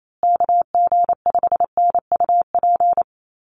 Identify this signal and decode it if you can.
Morse code, heard as a keyed tone, about 28 wpm, 700 Hz. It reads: KG5NUP
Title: USA Call Signs Repeated 2x 28wpm